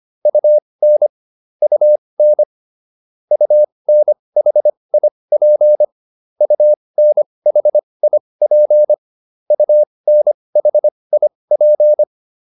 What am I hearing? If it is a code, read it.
Morse code, heard as a keyed tone, about 25 wpm, spaced as for 20 wpm, 600 Hz. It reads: UN UN UNHIP UNHIP UNHIP